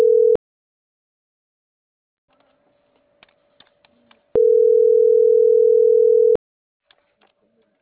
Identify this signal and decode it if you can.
Ringback tone (North America), 440 + 480 Hz.